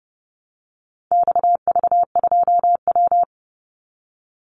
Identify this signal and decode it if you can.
Morse code, heard as a keyed tone, about 30 wpm, 700 Hz. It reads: XV2W
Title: Call Signs Repeated 3x - 30wpm